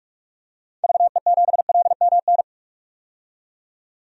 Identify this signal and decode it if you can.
Morse code, heard as a keyed tone, about 46 wpm, 700 Hz. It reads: VE7LMN
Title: DXCC-100 Call Signs - 6 Characters - 45wpm